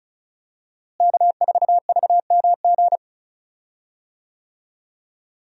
Morse code, heard as a keyed tone, about 36 wpm, 700 Hz. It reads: K4VMG
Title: USA Call Signs - 5 Characters - Repeated 2x - 35wpm